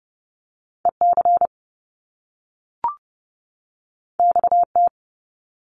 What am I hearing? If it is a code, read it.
Morse code, heard as a keyed tone, about 30 wpm, 700 Hz. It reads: EC XT
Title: Top Two Letter Combinations in Words 30wpm